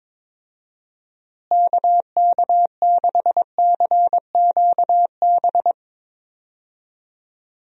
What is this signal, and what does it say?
Morse code, heard as a keyed tone, about 22 wpm, 700 Hz. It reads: KK6CQB
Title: USA Call Signs - 6 Characters - Repeated 2x - 22wpm